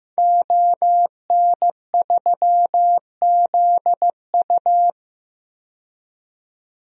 Morse code, heard as a keyed tone, about 15 wpm, 700 Hz. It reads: ON3ZU
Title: DXCC-100 Call Signs - 5 Characters - 15wpm